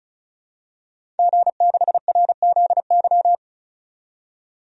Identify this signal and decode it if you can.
Morse code, heard as a keyed tone, about 36 wpm, 700 Hz. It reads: G6RZY
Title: DXCC-100 Call Signs - 5 Characters - 35wpm